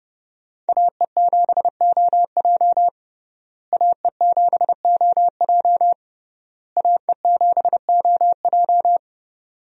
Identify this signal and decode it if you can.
Morse code, heard as a keyed tone, about 30 wpm, 700 Hz. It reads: AE7OJ AE7OJ AE7OJ
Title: USA Call Signs - 5 Characters - Repeated 3x - 30wpm